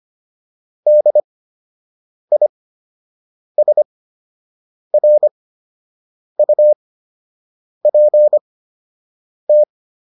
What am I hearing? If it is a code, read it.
Morse code, heard as a keyed tone, about 25 wpm, spaced as for 7 wpm, 600 Hz. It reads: DISRUPT
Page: fundamental-practice-files